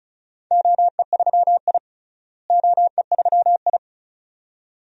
Morse code, heard as a keyed tone, about 36 wpm, 700 Hz. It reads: OE3I OE3I
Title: DXCC-100 Call Signs - 4 Characters - Repeated 2x - 35wpm